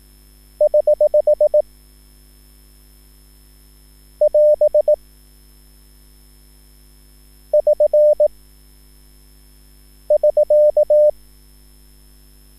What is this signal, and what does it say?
Morse code, heard as a keyed tone, about 18 wpm, 600 Hz. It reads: <HH> <AS> <SN> <SK>
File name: QSL / CW lesson